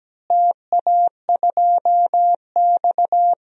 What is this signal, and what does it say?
Morse code, heard as a keyed tone, about 17 wpm, 700 Hz. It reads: TA2X
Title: DXCC-100 Call Signs - 4 Characters - Repeated 3x - 17wpm